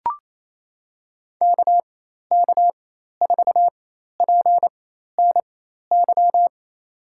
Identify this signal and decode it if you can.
Morse code, heard as a keyed tone, about 28 wpm, spaced as for 13 wpm, 700 Hz. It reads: KK4PNY
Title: USA Call Signs - 4x Character Spacing - 28wpm